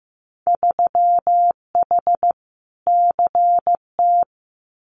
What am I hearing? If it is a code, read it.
Morse code, heard as a keyed tone, about 15 wpm, 700 Hz. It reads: 3H CT
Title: Winter Field Day Exchanges Encore 15wpm